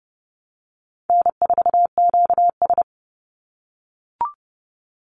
Morse code, heard as a keyed tone, about 30 wpm, 700 Hz. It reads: N4QS
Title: Easy Call Signs - 30wpm